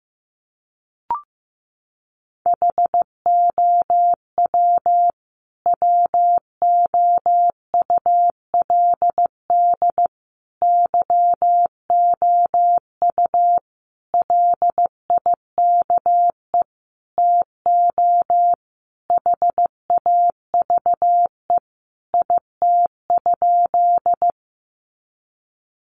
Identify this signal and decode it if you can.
Morse code, heard as a keyed tone, about 15 wpm, 700 Hz. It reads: HOW WOULD YOU LIKE TO HAVE IT?
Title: Sentences from Top 100 Words 15wpm